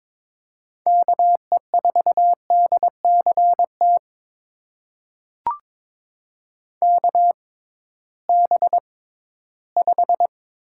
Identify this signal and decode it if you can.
Morse code, heard as a keyed tone, about 22 wpm, spaced as for 14 wpm, 700 Hz. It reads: KE4DCT KB5
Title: USA Call Signs - 6 Characters - 6x Character Spacing - 22wpm